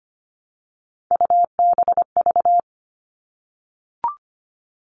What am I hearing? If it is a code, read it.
Morse code, heard as a keyed tone, about 25 wpm, 700 Hz. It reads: UBV
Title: All Cut Numbers - 3-Digit 25wpm